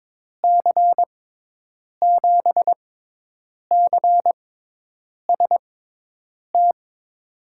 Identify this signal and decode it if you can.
Morse code, heard as a keyed tone, about 22 wpm, spaced as for 8 wpm, 700 Hz. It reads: C7CST
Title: USA Call Signs - 6 Characters - 6x Character Spacing - 22wpm